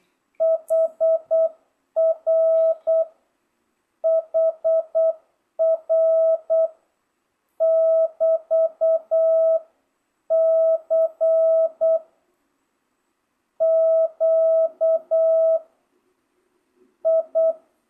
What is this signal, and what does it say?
Morse code, heard as a keyed tone, about 8 wpm, 660 Hz. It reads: HR HR <BT> C Q I